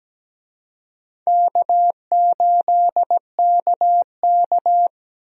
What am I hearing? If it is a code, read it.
Morse code, heard as a keyed tone, about 17 wpm, 700 Hz. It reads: K8KK
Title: USA Call Signs - 4 Characters - Repeated 2x - 17wpm